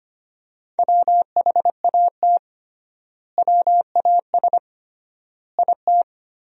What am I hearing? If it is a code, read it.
Morse code, heard as a keyed tone, about 25 wpm, 700 Hz. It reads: WHAT WAS IT
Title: 3x Word-Spacing - Sentences from Top 200 Words - 25wpm